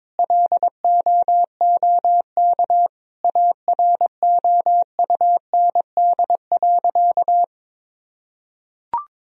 Morse code, heard as a keyed tone, about 22 wpm, 700 Hz. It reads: LOOK AROUND.
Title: Sentences from Top 1000 Words - Review Part 3 - 22wpm